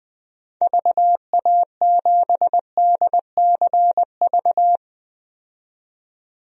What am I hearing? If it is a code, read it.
Morse code, heard as a keyed tone, about 20 wpm, 700 Hz. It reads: VA7DCV
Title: DXCC-100 Call Signs - 6 Characters - 20wpm